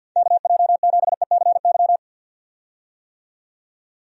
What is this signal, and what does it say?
Morse code, heard as a keyed tone, about 46 wpm, 700 Hz. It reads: KJ7ECY
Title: USA Call Signs - 6 Characters - 50wpm